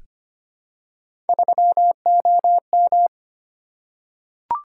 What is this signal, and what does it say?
Morse code, heard as a keyed tone, about 25 wpm, 700 Hz. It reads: 3OM
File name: Three Letter-Number Combinations - 25wpm